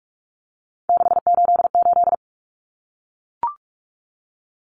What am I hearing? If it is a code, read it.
Morse code, heard as a keyed tone, about 44 wpm, 700 Hz. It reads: 688
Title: Numbers - 3-Digit 45wpm